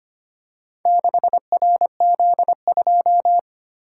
Morse code, heard as a keyed tone, about 25 wpm, 700 Hz. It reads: 6RZ2
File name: practice / 25wpm / Four Letter-Number Combinations - 25wpm